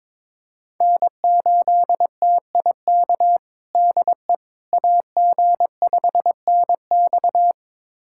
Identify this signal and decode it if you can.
Morse code, heard as a keyed tone, about 22 wpm, 700 Hz. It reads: N8TIK DE AG5NX
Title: USA Callsign DE Callsign - 5 Characters - 22wpm